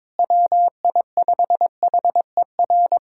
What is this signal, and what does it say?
Morse code, heard as a keyed tone, about 22 wpm, 700 Hz. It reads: WI5HER
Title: USA Callsign DE Callsign 22wpm